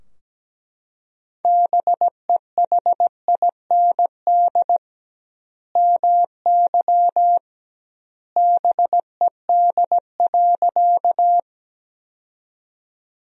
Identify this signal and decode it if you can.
Morse code, heard as a keyed tone, about 17 wpm, 700 Hz. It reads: BEHIND MY BED.
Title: 2x Word-Spacing - Sentences from Top 500 Words - 17wpm